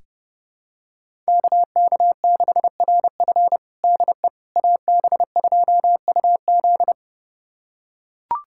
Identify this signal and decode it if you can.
Morse code, heard as a keyed tone, about 30 wpm, 700 Hz. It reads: KK6RF DE AB2UZ
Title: USA Callsign DE Callsign - 5 Characters - 30wpm